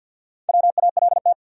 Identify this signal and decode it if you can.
Morse code, heard as a keyed tone, about 53 wpm, 700 Hz. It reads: WALT
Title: USA Common Names - 50wpm